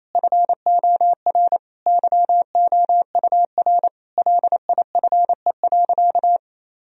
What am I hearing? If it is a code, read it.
Morse code, heard as a keyed tone, about 28 wpm, 700 Hz. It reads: FOR YOUR LIFE.
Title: Sentences from Top 1000 Words - Review Part 2 - 28wpm